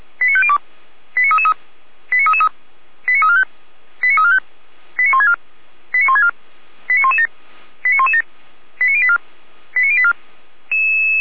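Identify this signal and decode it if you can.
Five-tone selective call (ZVEI), heard as quick runs of five tones, tones of about 70 ms. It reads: 89502 89333 89333 89345 89345 89164 89164 89117 89117 89084 89084
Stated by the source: ZVEI five-tone alert received by SDR, audio.f85360.feuerwehr-orlamuende.org fire brigade server.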